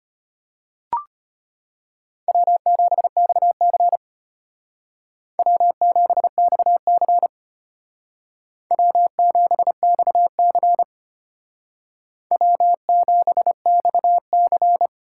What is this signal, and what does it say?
Morse code, heard as a keyed tone, about 32 wpm, 700 Hz. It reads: W7XC W7XC W7XC W7XC